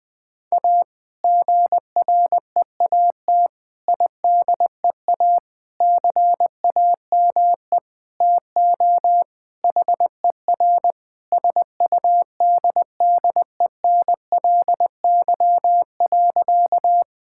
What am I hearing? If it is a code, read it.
Morse code, heard as a keyed tone, about 20 wpm, 700 Hz. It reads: A GREAT IDEA CAME TO HER SUDDENLY.
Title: Sentences with 2-Word Phrases - Spoken Beginning - 20wpm